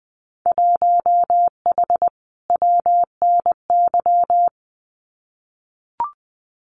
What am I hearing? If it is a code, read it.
Morse code, heard as a keyed tone, about 20 wpm, 700 Hz. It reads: 1H WNY
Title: Winter Field Day Exchanges Encore 20wpm